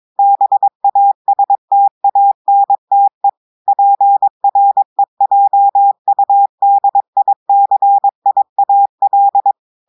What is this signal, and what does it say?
Morse code, heard as a keyed tone, about 22 wpm, 800 Hz. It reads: BASTANTE PREJUDICIAL
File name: audios / portugais / portugais texte en clair 1 vitesse 22 mots minutes